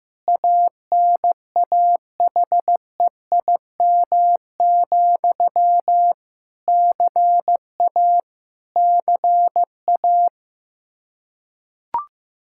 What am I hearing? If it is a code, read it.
Morse code, heard as a keyed tone, about 15 wpm, 700 Hz. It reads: ANAHEIM, CA CA
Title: Practice QSO - Exchange 1 - Beginning - 15wpm